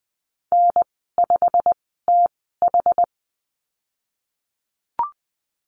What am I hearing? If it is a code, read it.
Morse code, heard as a keyed tone, about 20 wpm, spaced as for 15 wpm, 700 Hz. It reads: N5TH
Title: 122 Lesson 20g - Number 3 - Call Signs 20wpm